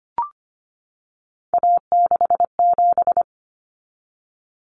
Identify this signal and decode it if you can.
Morse code, heard as a keyed tone, about 25 wpm, 700 Hz. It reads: A67
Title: Common Cut Numbers - 3-Digit 25wpm